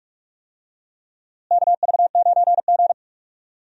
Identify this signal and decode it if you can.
Morse code, heard as a keyed tone, about 41 wpm, 700 Hz. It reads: KV9G
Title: USA Call Signs - 4 Characters - Repeated 3x - 45wpm